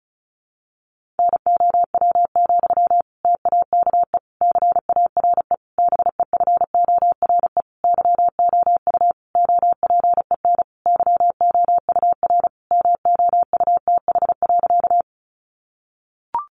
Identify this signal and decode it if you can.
Morse code, heard as a keyed tone, about 34 wpm, 700 Hz. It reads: NOW, TAKE CARE BEFORE YOU OPEN YOUR MOUTH.
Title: Sentences from Top 1500 Words 35wpm